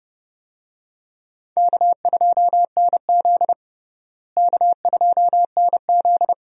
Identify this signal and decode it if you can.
Morse code, heard as a keyed tone, about 30 wpm, 700 Hz. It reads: K2NZ K2NZ
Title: USA Call Signs - 4 Characters - Repeated 3x - 30wpm